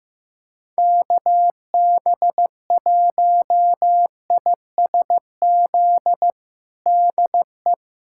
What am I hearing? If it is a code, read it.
Morse code, heard as a keyed tone, about 15 wpm, 700 Hz. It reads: KB1ISZ DE
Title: USA Callsign DE Callsign 15wpm